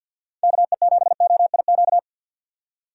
Morse code, heard as a keyed tone, about 53 wpm, 700 Hz. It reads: KE7OIQ
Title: USA Call Signs - 4x Character Spacing - 50wpm